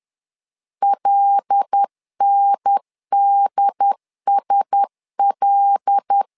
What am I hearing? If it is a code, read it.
Morse code, heard as a keyed tone, about 10 wpm, 800 Hz. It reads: LNDSL
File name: LEZIONI / 5 LBVC / 5 L 10wpm 2259